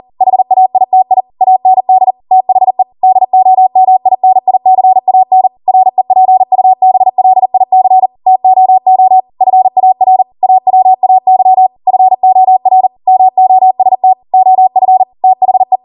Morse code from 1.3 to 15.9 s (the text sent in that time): AND THE DOMINICAN REPUBLIC TOO FAR AWAY FOR MOST OF THE